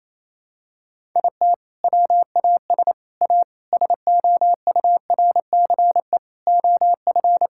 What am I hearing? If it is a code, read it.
Morse code, heard as a keyed tone, about 28 wpm, 700 Hz. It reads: IT WAS A SOURCE OF
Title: Sentences from Top 1000 Words - Review Part 5 - 28wpm